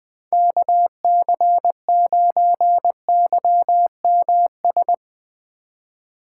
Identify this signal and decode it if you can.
Morse code, heard as a keyed tone, about 20 wpm, 700 Hz. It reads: KC9YMS
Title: USA Call Signs - 6 Characters - 8x Character Spacing - 20wpm